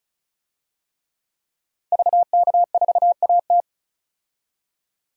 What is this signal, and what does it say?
Morse code, heard as a keyed tone, about 34 wpm, 700 Hz. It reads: VK4AT
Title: DXCC-100 Call Signs - 5 Characters - 35wpm